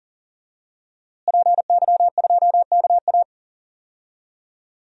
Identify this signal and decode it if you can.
Morse code, heard as a keyed tone, about 37 wpm, 700 Hz. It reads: PY2KA
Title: DXCC-100 Call Signs - 5 Characters - Repeated 2x - 40wpm